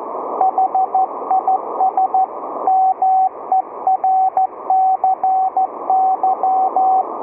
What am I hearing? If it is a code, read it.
Morse code, heard as a keyed tone, about 14 wpm, 760 Hz, with radio noise. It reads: HIS MERCY